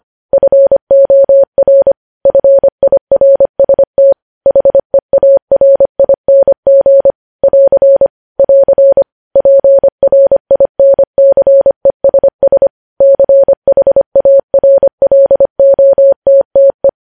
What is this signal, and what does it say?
Morse code, heard as a keyed tone, about 25 wpm, 550 Hz. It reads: FOR FIRST HEARING <AR> <AR> PRINCESS CHARLOTTE
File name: podcast / News headlines-1x-25x25